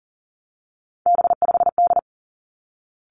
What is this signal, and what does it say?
Morse code, heard as a keyed tone, about 42 wpm, 700 Hz. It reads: B5D
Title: Three Challenging Letter-Number Combinations 40wpm